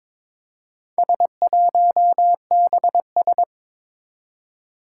Morse code, heard as a keyed tone, about 22 wpm, 700 Hz. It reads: S1BS
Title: Four Challenging Letter-Number Combinations 22wpm